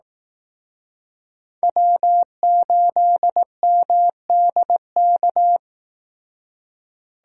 Morse code, heard as a keyed tone, about 18 wpm, 700 Hz. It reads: W8MDK
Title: Speed Racing - USA 1x3 Call Signs 15wpm